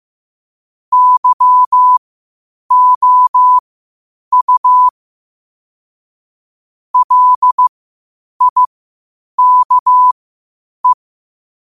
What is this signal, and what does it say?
Morse code, heard as a keyed tone, about 15 wpm, spaced as for 8 wpm, 1000 Hz. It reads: YOU LIKE
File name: qotdfiles / 8wpm